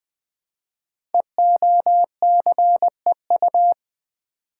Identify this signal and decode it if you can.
Morse code, heard as a keyed tone, about 20 wpm, 700 Hz. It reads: EOCEU
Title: Code Groups - 5 Letter:Numbers - 20wpm